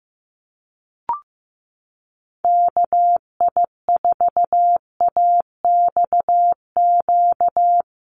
Morse code, heard as a keyed tone, about 15 wpm, 700 Hz. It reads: KI4AXQ
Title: USA Call Signs 15wpm